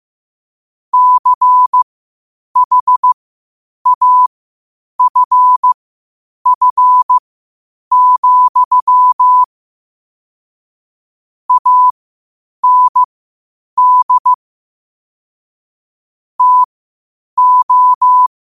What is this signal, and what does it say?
Morse code, heard as a keyed tone, about 15 wpm, spaced as for 8 wpm, 1000 Hz. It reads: CHAFF, AND TO
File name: qotdfiles / 8wpm